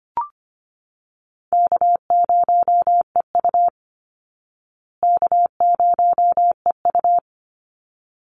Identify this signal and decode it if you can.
Morse code, heard as a keyed tone, about 25 wpm, 700 Hz. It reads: K0EU K0EU
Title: Easy Call Signs Repeated 2x - 25wpm